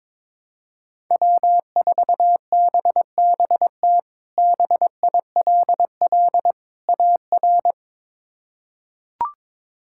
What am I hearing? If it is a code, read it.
Morse code, heard as a keyed tone, about 22 wpm, 700 Hz. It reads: W4BBT BILL AR
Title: K1USN SST QSO Practice - 22wpm